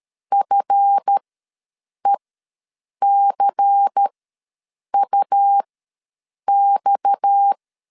Morse code, heard as a keyed tone, about 13 wpm, spaced as for 7 wpm, 800 Hz. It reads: FECUX
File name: LEZIONI / 7 GF9Z / 7 GF ws 12wpm 3140